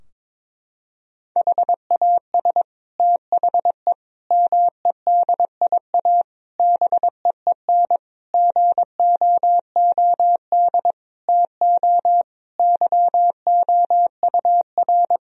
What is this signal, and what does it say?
Morse code, heard as a keyed tone, about 22 wpm, 700 Hz. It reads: HAS THE MEDIA BEEN GOOD TO YOUR
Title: Sentences from Top 1000 Words - Review Part 4 - 22wpm